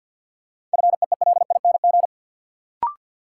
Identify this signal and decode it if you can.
Morse code, heard as a keyed tone, about 52 wpm, 700 Hz. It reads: FEELING
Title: Sentences from Top 1200 Words - New Word Drill Alt - 50wpm